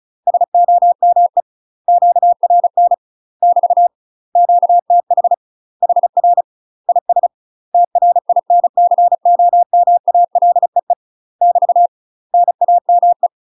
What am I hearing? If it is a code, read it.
Morse code, heard as a keyed tone, about 35 wpm, 700 Hz. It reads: SOME QRN <BT> QTH HR IS TRINCOMALEE <BT> NAME
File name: UK MASTER QSOs TRAINING/35WPM